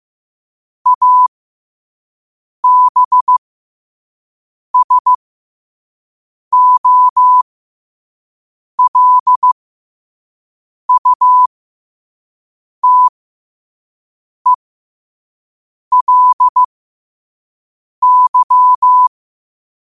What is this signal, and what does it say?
Morse code, heard as a keyed tone, about 15 wpm, spaced as for 5 wpm, 1000 Hz. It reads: ABSOLUTELY
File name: qotdfiles / 5wpm